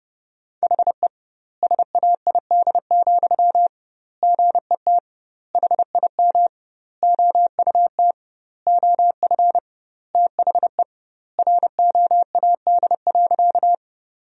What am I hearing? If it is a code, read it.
Morse code, heard as a keyed tone, about 32 wpm, 700 Hz. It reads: HE SAID, GET HIM OUT OF THE ROAD.
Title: 2x Word-Spacing - Sentences from Top 400 Words - 30wpm